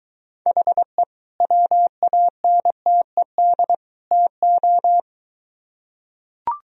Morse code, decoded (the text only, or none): HE WANTED TO